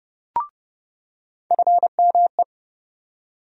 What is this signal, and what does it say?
Morse code, heard as a keyed tone, about 30 wpm, 700 Hz. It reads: FME
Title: USA Call Sign Postfixes - 3 Letters 30wpm